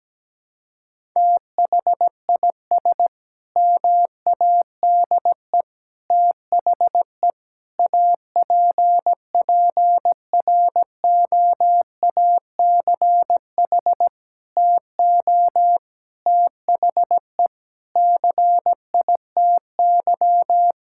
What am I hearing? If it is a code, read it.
Morse code, heard as a keyed tone, about 17 wpm, 700 Hz. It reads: THIS MADE THE APPROACH TO THE CITY
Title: Sentences from Top 1000 Words - Review Part 6 - 17wpm